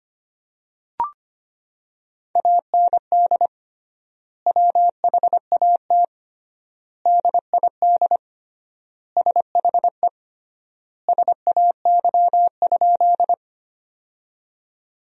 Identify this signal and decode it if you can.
Morse code, heard as a keyed tone, about 25 wpm, 700 Hz. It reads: AND WHAT DID SHE SAY?
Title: 3x Word-Spacing - Sentences from Top 200 Words - 25wpm